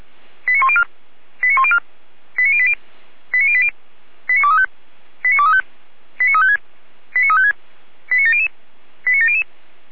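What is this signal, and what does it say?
Five-tone selective call (ZVEI), heard as quick runs of five tones, tones of about 70 ms. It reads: 89104 89104 89080 89080 89235 89235 89356 89356 89700 89700
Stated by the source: ZVEI five-tone selective call 89262 received by SDR, fire brigade feuerwehr-orlamuende.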